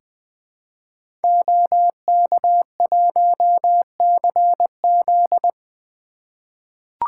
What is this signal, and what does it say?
Morse code, heard as a keyed tone, about 20 wpm, 700 Hz. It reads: OK1CZ
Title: 2023 SOTA Activator Call Signs - 20wpm